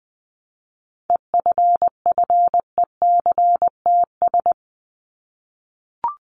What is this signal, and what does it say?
Morse code, heard as a keyed tone, about 20 wpm, 700 Hz. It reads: EFFECTS
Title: ES:S Suffix - Top 100 Words with Base Words 20wpm